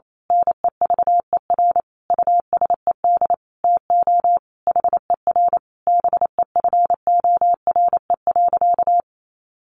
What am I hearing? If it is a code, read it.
Morse code, heard as a keyed tone, about 28 wpm, 700 Hz. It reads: NEVER USED TO HER BEFORE.
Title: Sentences from Top 600 Words 28wpm